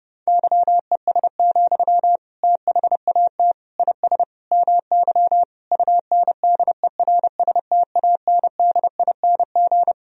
Morse code, heard as a keyed tone, about 32 wpm, 700 Hz. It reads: YES, THAT IS MY UNDERSTANDING
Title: Sentences from Top 1000 Words - Review Part 3 - 30wpm